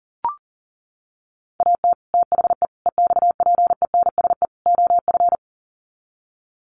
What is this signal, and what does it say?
Morse code, heard as a keyed tone, about 40 wpm, 700 Hz. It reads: AT THE EXPENSE OF
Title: Sets of 4 Words 40wpm